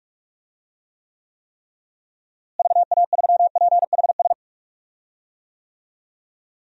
Morse code, heard as a keyed tone, about 46 wpm, 700 Hz. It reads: VA3PHS
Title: DXCC-100 Call Signs - 6 Characters - Repeated 3x - 45wpm